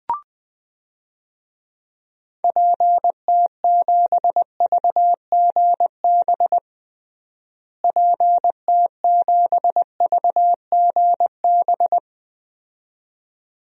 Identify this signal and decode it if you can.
Morse code, heard as a keyed tone, about 20 wpm, 700 Hz. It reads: PT7VGB PT7VGB
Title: DXCC-100 Call Signs - 6 Characters - Repeated 2x - 20wpm